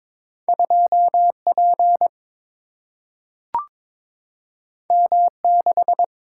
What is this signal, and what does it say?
Morse code, heard as a keyed tone, about 22 wpm, 700 Hz. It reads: 2 P M 6
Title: Single-Letter Single-Number 22wpm